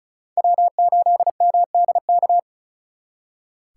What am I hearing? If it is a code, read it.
Morse code, heard as a keyed tone, about 46 wpm, 700 Hz. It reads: W8MDK